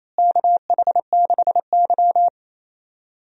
Morse code, heard as a keyed tone, about 28 wpm, 700 Hz. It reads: KH6Y
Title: DXCC-100 Call Signs - 4 Characters - Repeated 2x - 28wpm